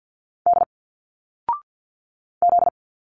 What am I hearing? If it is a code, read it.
Morse code, heard as a keyed tone, about 50 wpm, 700 Hz. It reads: D Z
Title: Single Letters 50wpm